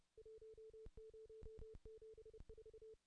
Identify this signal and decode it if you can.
Morse code, heard as a keyed tone, about 27 wpm, 440 Hz. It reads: 1074
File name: contests / ss-cw-k5tr-op-n5rz -1500